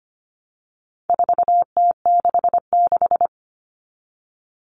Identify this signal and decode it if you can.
Morse code, heard as a keyed tone, about 25 wpm, 700 Hz. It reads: 4T66
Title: All Cut Numbers - 4-Digit 25wpm